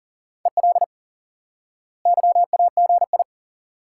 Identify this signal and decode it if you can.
Morse code, heard as a keyed tone, about 38 wpm, 700 Hz. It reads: EL YAGI
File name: IQE Course 3 - Lesson 89 - YAGI - Two Elements - 40wpm